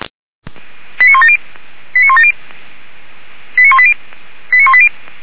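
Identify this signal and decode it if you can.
Five-tone selective call (ZVEI), heard as quick runs of five tones, tones of about 70 ms. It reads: 89170 89170 89180 89180
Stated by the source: ZVEI five-tone selective call 89110 received via SDR.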